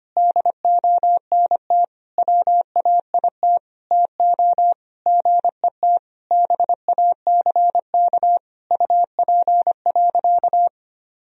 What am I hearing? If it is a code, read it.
Morse code, heard as a keyed tone, about 25 wpm, 700 Hz. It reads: DONT WAIT TO GET BACK UP.
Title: Sentences from Top 1000 Words - Review Part 2 - 25wpm